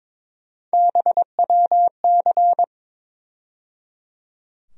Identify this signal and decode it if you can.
Morse code, heard as a keyed tone, about 22 wpm, 700 Hz. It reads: BWC
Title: USA Call Sign Postfixes - 3 Letters 22wpm